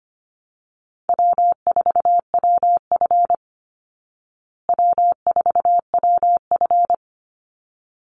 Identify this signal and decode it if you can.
Morse code, heard as a keyed tone, about 25 wpm, 700 Hz. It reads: W4WF W4WF
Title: Easy Call Signs Repeated 2x - 25wpm